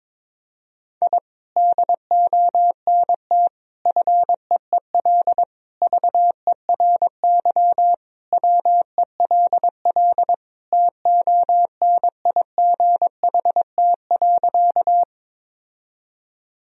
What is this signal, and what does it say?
Morse code, heard as a keyed tone, about 22 wpm, 700 Hz. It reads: I DONT FEEL VERY WELL TONIGHT.
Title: Sentences from Top 1000 Words - Review Part 3 - 22wpm